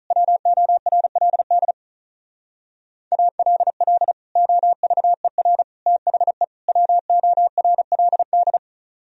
Morse code, heard as a keyed tone, about 43 wpm, 700 Hz. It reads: WORLD ALL OVER THE WORLD
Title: Speed-Racing - Sets of 4 Words 35wpm